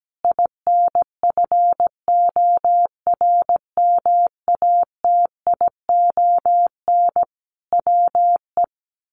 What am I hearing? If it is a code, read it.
Morse code, heard as a keyed tone, about 17 wpm, 700 Hz. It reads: INFORMATION WE
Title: Sentences from Top 1300 Words 17wpm